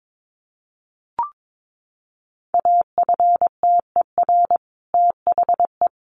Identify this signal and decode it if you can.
Morse code, heard as a keyed tone, about 22 wpm, 700 Hz. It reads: AFTER THE
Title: A Sound Of Thunder 22wpm